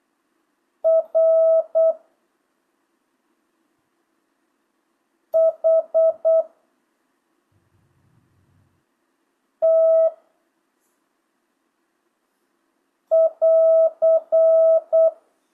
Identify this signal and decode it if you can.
Morse code, heard as a keyed tone, about 8 wpm, 650 Hz. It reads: R H T <AR>
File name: jh8cbh / 2025cw